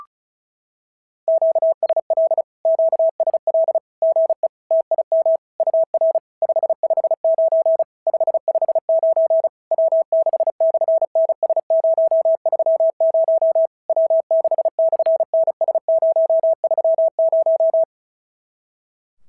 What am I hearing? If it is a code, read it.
Morse code, heard as a keyed tone, about 33 wpm, 650 Hz. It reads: QSL QSL GE TIM UR 559 559 W6/NS030 W6/NS030
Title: SOTA QSO Practice 35wpm